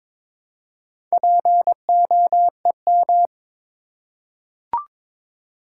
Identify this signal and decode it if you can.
Morse code, heard as a keyed tone, about 22 wpm, 700 Hz. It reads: POEM
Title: Four Letter Words - 22wpm